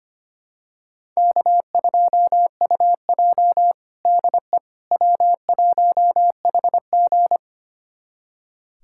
Morse code, heard as a keyed tone, about 25 wpm, 700 Hz. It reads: K2UJ DE W1HG
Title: USA Callsign DE Callsign - 4 Characters - 25wpm